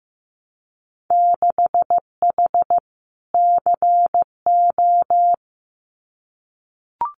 Morse code, heard as a keyed tone, about 15 wpm, 700 Hz. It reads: 6H CO
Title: Winter Field Day Exchanges Encore 15wpm